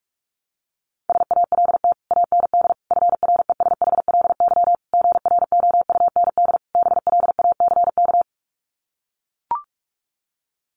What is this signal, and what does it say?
Morse code, heard as a keyed tone, about 44 wpm, 700 Hz. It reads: SALT AND FRESHLY GROUND BLACK
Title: Sets of 5 Words Encore 45wpm